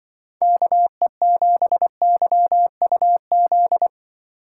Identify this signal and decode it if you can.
Morse code, heard as a keyed tone, about 25 wpm, 700 Hz. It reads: KE7YUZ
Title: Speed Racing - USA 2x3 Call Signs 20wpm